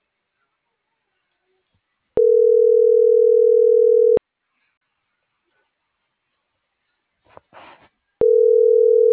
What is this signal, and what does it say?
Ringback tone (North America), 440 + 480 Hz.